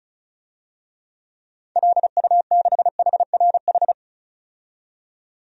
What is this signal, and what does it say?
Morse code, heard as a keyed tone, about 34 wpm, 700 Hz. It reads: LU6HRH
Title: DXCC-100 Call Signs - 6 Characters - 35wpm